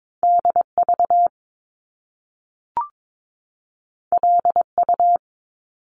Morse code, heard as a keyed tone, about 22 wpm, 700 Hz. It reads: DV LU
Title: Top Two Letter Combinations in Words 22wpm